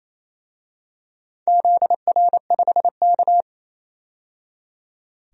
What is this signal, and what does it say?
Morse code, heard as a keyed tone, about 27 wpm, 700 Hz. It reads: ZR5K
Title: DXCC-100 Call Signs - 4 Characters - 28wpm